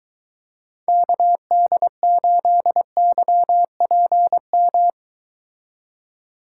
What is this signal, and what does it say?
Morse code, heard as a keyed tone, about 23 wpm, 700 Hz. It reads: KD8YPM